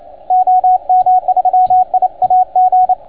Morse code, heard as a keyed tone, about 28 wpm, 690 Hz, with radio noise. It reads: OM3IAG
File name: Archiv / CUC287 / OK1FGD CUC 287